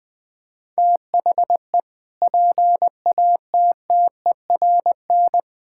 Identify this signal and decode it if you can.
Morse code, heard as a keyed tone, about 19 wpm, 700 Hz. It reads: THE PATTERN
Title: Sentences with 6-Word Phrases - 20wpm